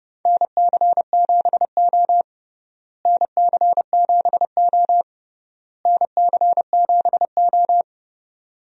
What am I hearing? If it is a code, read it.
Morse code, heard as a keyed tone, about 30 wpm, 700 Hz. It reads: NC7O NC7O NC7O
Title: USA Call Signs - 4 Characters - Repeated 3x - 30wpm